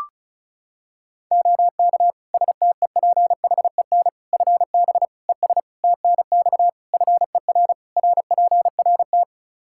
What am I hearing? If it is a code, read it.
Morse code, heard as a keyed tone, about 34 wpm, 700 Hz. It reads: OK STEPHEN FB ES TNX FER RPRT
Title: Practice QSO - Exchange 2 - Info - 35wpm